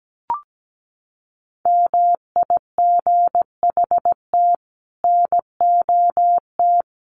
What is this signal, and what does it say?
Morse code, heard as a keyed tone, about 17 wpm, 700 Hz. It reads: MIGHT NOT
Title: Sets of 5 Words Encore2 17wpm